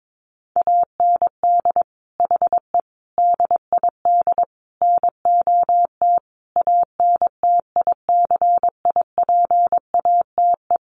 Morse code, heard as a keyed tone, about 22 wpm, 700 Hz. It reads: AND HE DID NOT ANTICIPATE
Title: Sentences from Top 1500 Words 22wpm